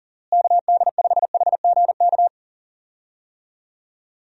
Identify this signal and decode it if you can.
Morse code, heard as a keyed tone, about 40 wpm, 700 Hz. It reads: KD5HGK
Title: USA Call Signs - 6 Characters - Repeated 3x - 40wpm